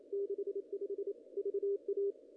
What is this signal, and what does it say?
Morse code, heard as a keyed tone, about 27 wpm, 410 Hz. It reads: 65 VA
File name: contests / ss-cw-k5tr-op-n5rz -0000